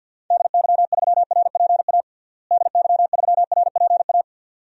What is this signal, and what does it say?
Morse code, heard as a keyed tone, about 53 wpm, 700 Hz. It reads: DY3RPA DY3RPA
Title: DXCC-100 Call Signs - 6 Characters - Repeated 2x - 50wpm